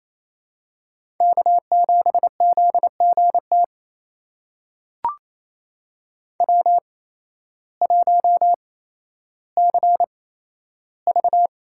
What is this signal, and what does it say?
Morse code, heard as a keyed tone, about 28 wpm, spaced as for 13 wpm, 700 Hz. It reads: K7ZGT W1CV
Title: USA Call Signs - 5 Characters - 8x Character Spacing - 28wpm